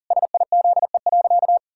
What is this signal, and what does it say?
Morse code, heard as a keyed tone, about 40 wpm, 700 Hz. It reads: SIZE.
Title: Pangram Challenge v2 - 40wpm